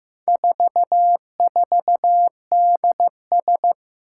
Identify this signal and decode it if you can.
Morse code, heard as a keyed tone, about 15 wpm, 700 Hz. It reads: 44DS
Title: Four Challenging Letter-Number Combinations 15wpm